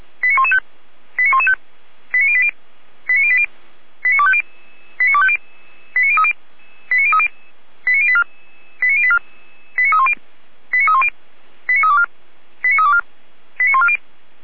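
Five-tone selective call (ZVEI), heard as quick runs of five tones, tones of about 70 ms. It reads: 89115 89115 89080 89080 89250 89250 89030 89030 89084 89084 89310 89310 89324 89324 89140
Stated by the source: ZVEI five-tone selective call received with an SDR.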